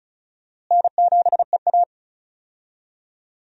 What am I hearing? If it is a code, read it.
Morse code, heard as a keyed tone, about 34 wpm, 700 Hz. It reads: N7EA
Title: USA Call Signs - 4 Characters - Repeated 2x - 35wpm